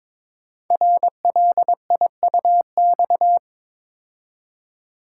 Morse code, heard as a keyed tone, about 23 wpm, 700 Hz. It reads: RLIUX
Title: Code Groups - 5 Letter:Numbers - 22wpm